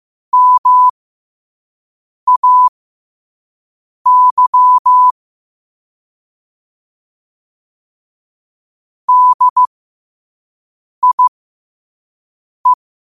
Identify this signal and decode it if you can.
Morse code, heard as a keyed tone, about 15 wpm, spaced as for 5 wpm, 1000 Hz. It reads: MAY DIE
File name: qotdfiles / 5wpm